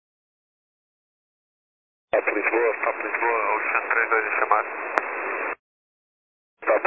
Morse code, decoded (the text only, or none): none